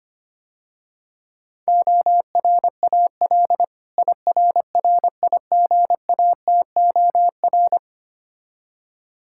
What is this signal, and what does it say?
Morse code, heard as a keyed tone, about 25 wpm, 700 Hz. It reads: ORAL IRRIGATOR
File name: practice / 25wpm / Difficult Words - Speed-Racing with Hints - 25wpm